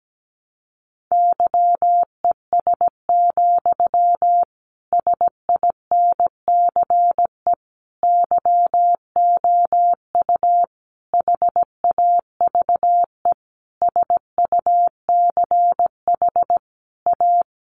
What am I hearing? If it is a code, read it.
Morse code, heard as a keyed tone, about 17 wpm, 700 Hz. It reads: YES, SINCE YOU HAVE SUCH A
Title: Sentences from Top 1300 Words 17wpm